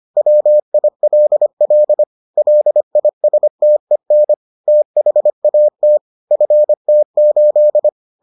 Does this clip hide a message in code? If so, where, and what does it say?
Yes: Morse code from 0.2 to 7.9 s, heard as a keyed tone, about 25 wpm, 600 Hz. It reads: WILL LISTEN THAT FT8